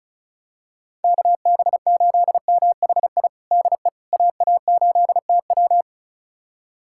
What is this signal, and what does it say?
Morse code, heard as a keyed tone, about 36 wpm, 700 Hz. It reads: KB8MHI DE AA8TW
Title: USA Callsign DE Callsign 35wpm